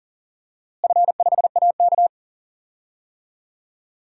Morse code, heard as a keyed tone, about 40 wpm, 700 Hz. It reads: F5AK
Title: DXCC-100 Call Signs - 4 Characters - Repeated 2x - 40wpm